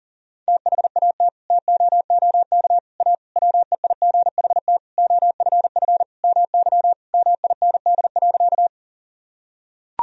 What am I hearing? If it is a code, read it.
Morse code, heard as a keyed tone, about 40 wpm, 700 Hz. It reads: THAT TOOK A WEIGHT OFF MY MIND.
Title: Sentences from Top 1000 Words - Review Part 3 - 40wpm